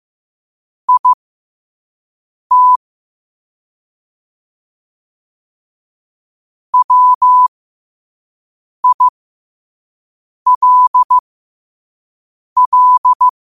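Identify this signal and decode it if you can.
Morse code, heard as a keyed tone, about 15 wpm, spaced as for 5 wpm, 1000 Hz. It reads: IT WILL
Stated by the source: Morse code quote of the day sent at 5 wpm.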